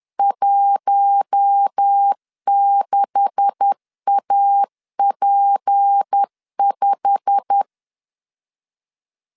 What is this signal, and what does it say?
Morse code, heard as a keyed tone, about 10 wpm, 800 Hz. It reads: 16AP5
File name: LEZIONI / 10 SIGNS / 10A SLASH 10wpm 2808